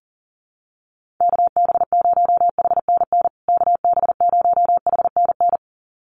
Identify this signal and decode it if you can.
Morse code, heard as a keyed tone, about 40 wpm, 700 Hz. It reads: KB0HNN KB0HNN
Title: USA Call Signs Repeated 2x 40wpm